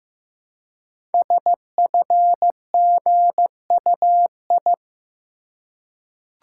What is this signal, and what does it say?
Morse code, heard as a keyed tone, about 15 wpm, 700 Hz. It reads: SFGUI
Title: Code Groups - 5 Letter:Numbers - 15wpm